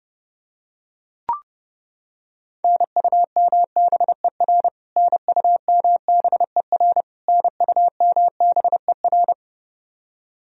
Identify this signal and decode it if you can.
Morse code, heard as a keyed tone, about 30 wpm, 700 Hz. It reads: NUMBER NUMBER NUMBER
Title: Top 100 Words Repeated 30wpm